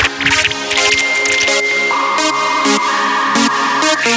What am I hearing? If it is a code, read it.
Dial-up modem handshake.